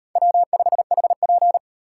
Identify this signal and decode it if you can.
Morse code, heard as a keyed tone, about 36 wpm, 700 Hz. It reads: W5HP